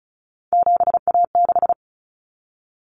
Morse code, heard as a keyed tone, about 34 wpm, 700 Hz. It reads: 7A6
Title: Common Cut Numbers - 3-Digit 35wpm